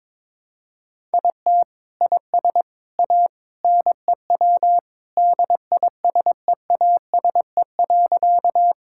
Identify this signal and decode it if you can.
Morse code, heard as a keyed tone, about 22 wpm, 700 Hz. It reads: IT IS A NEW DISEASE.
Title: Sentences from Top 1000 Words - Review Part 6 - 22wpm